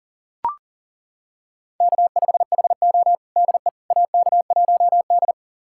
Morse code, heard as a keyed tone, about 37 wpm, 700 Hz. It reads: K5HO DE AK1D
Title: USA Callsign DE Callsign - 4 Characters - 40wpm